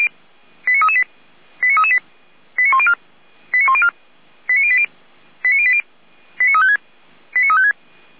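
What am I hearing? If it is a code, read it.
Five-tone selective call (ZVEI), heard as quick runs of five tones, tones of about 70 ms. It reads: 89338 89338 89104 89104 89080 89080 89356 89356
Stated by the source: ZVEI five-tone selective call 89126 received on an SDR.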